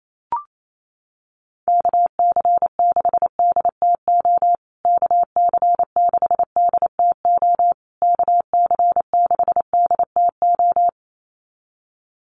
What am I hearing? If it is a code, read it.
Morse code, heard as a keyed tone, about 28 wpm, 700 Hz. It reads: KC6DTO KC6DTO KC6DTO
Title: USA Call Signs Repeated 3x 28wpm